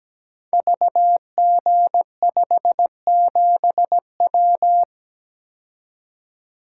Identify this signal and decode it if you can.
Morse code, heard as a keyed tone, about 17 wpm, 700 Hz. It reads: VG57W
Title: Code Groups - 5 Letter:Numbers - 17wpm